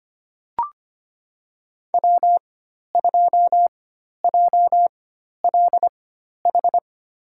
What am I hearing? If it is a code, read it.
Morse code, heard as a keyed tone, about 25 wpm, spaced as for 12 wpm, 700 Hz. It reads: W2JLH
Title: USA Call Signs - 5 Characters - 4x Character Spacing - 25wpm